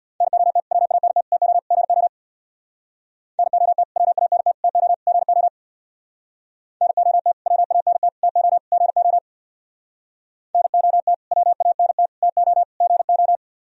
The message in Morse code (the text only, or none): NOT WANT TO GO NOT WANT TO GO NOT WANT TO GO NOT WANT TO GO